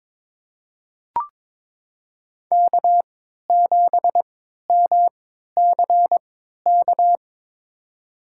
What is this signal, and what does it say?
Morse code, heard as a keyed tone, about 22 wpm, spaced as for 12 wpm, 700 Hz. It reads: K7MCK
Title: USA Call Signs - 5 Characters - 3x Character Spacing - 22wpm